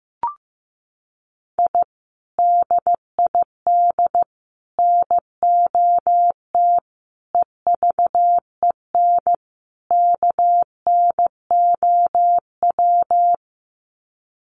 Morse code, heard as a keyed tone, about 15 wpm, 700 Hz. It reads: I DID NOT EVEN KNOW
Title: Sets of 5 Words Encore 15wpm